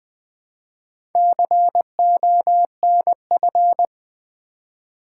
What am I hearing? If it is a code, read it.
Morse code, heard as a keyed tone, about 21 wpm, 700 Hz. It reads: CONF
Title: Build A Word - 4 Letters At A Time - 10-Letter Words - 20wpm